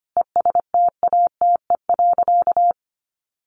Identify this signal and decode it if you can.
Morse code, heard as a keyed tone, about 25 wpm, 700 Hz. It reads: ESTATE.
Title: Sentences from Top 1200 Words 25wpm